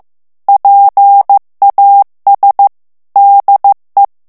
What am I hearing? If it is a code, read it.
Morse code, heard as a keyed tone, about 15 wpm, 800 Hz. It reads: PAS DE
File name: morse mp3 / FRANCAIS TEXTE CLAIR 2 VITESSE 14 MOTS MINUTES